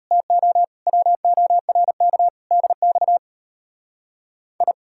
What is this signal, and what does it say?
Morse code, heard as a keyed tone, about 37 wpm, 700 Hz. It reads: TO WORK DX I